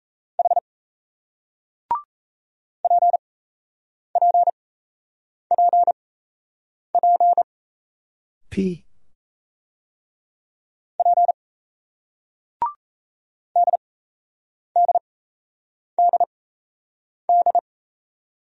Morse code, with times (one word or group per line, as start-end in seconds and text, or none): 0.4-0.6 H
2.8-7.4 PPPP
11.0-11.3 P
13.6-17.6 DDDD